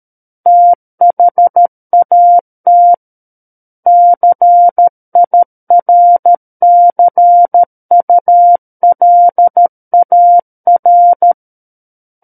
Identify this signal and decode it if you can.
Morse code, heard as a keyed tone, about 13 wpm, 700 Hz. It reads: THAT CIRCULAR
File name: Book One / 04-Chapter 4 - 13 WPM